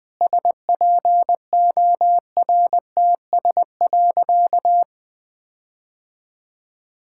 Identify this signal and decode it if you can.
Morse code, heard as a keyed tone, about 21 wpm, 700 Hz. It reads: SPORTS.
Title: Sentences with 6-Word Phrases - 20wpm